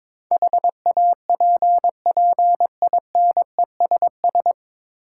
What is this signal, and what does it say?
Morse code, heard as a keyed tone, about 22 wpm, 700 Hz. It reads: HAPPINESS
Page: Morse Code Ninja practice 22wpm